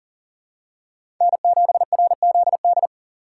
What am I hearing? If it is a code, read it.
Morse code, heard as a keyed tone, about 40 wpm, 700 Hz. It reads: N7RZD
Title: USA Call Signs - 5 Characters - 4x Character Spacing - 40wpm